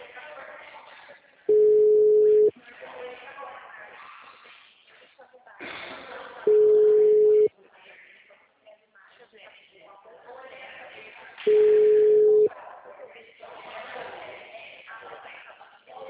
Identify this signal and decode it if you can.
Ringback tone (Europe, 425 Hz), with speech.
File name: out- -2005- - - .29846